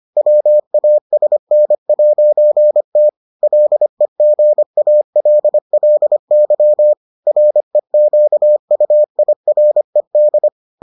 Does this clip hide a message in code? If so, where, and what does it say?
Yes: Morse code from 0.2 to 10.5 s, heard as a keyed tone, about 25 wpm, 600 Hz. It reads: WASN'T LEGALLY REQUIRED